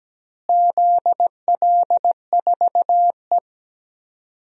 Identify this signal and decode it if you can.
Morse code, heard as a keyed tone, about 17 wpm, 700 Hz. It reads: ZL4E
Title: DXCC-100 Call Signs - 4 Characters - Repeated 2x - 17wpm